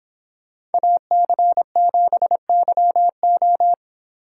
Morse code, heard as a keyed tone, about 25 wpm, 700 Hz. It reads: AC7YO